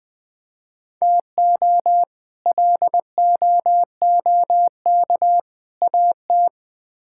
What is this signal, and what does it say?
Morse code, heard as a keyed tone, about 20 wpm, 690 Hz. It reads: TO LOOK AT
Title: Build A Phrase - 4-Word Phrases - 20wpm